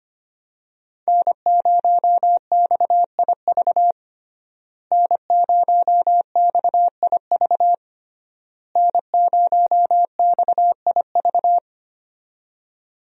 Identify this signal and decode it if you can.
Morse code, heard as a keyed tone, about 25 wpm, 700 Hz. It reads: N0XIV N0XIV N0XIV
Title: USA Call Signs - 5 Characters - Repeated 3x - 25wpm